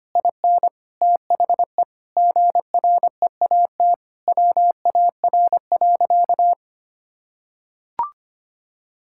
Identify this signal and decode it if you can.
Morse code, heard as a keyed tone, about 25 wpm, 700 Hz. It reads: IN THE GREAT WAR.
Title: Sentences from Top 1600 Words 25wpm